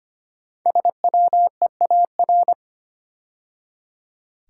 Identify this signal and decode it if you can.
Morse code, heard as a keyed tone, about 25 wpm, 700 Hz. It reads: SWEAR
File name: 5 Letter Words - First 3 Letters Given - 25wpm